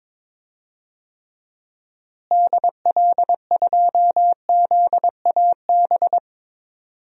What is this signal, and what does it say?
Morse code, heard as a keyed tone, about 22 wpm, 700 Hz. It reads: DL2ZAB
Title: DXCC-100 Call Signs - 6 Characters - Repeated 3x - 22wpm